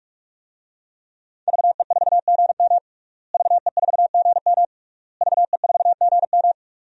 Morse code, heard as a keyed tone, about 42 wpm, 700 Hz. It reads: VE4GM VE4GM VE4GM
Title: DXCC-100 Call Signs - 5 Characters - Repeated 3x - 45wpm